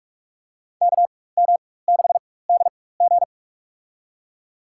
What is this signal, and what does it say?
Morse code, heard as a keyed tone, about 42 wpm, spaced as for 20 wpm, 700 Hz. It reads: KM6DG
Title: USA Call Signs - 4x Character Spacing - 45wpm